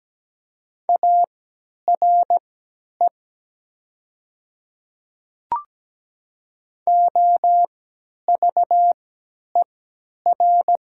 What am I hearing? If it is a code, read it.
Morse code, heard as a keyed tone, about 17 wpm, spaced as for 10 wpm, 700 Hz. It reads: ARE OVER
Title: 3x Character Spacing - Top 200 Words - 17wpm